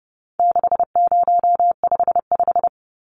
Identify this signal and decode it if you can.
Morse code, heard as a keyed tone, about 30 wpm, 700 Hz. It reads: 6055
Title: Numbers - 4-Digit 30wpm